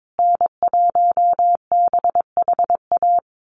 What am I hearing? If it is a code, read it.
Morse code, heard as a keyed tone, about 22 wpm, 700 Hz. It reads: N1BHA
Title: USA Call Signs Repeated 3x 22wpm